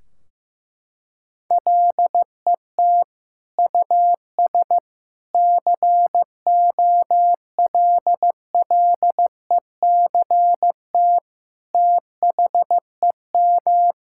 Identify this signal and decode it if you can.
Morse code, heard as a keyed tone, about 16 wpm, 700 Hz. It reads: LET US COLLECT THEM
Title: Sentences from Top 1600 Words 15wpm